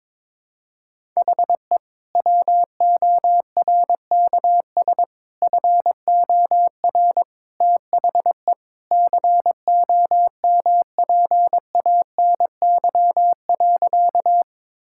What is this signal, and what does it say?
Morse code, heard as a keyed tone, about 22 wpm, 700 Hz. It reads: HE WORKS FOR THE COMPANY.
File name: practice / 22wpm / Sentences with 2-Word Phrases - 22wpm